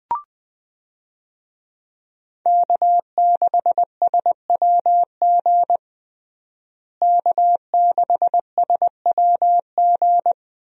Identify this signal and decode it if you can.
Morse code, heard as a keyed tone, about 20 wpm, 700 Hz. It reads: K6SWG K6SWG
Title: USA Call Signs - 5 Characters - Repeated 3x - 20wpm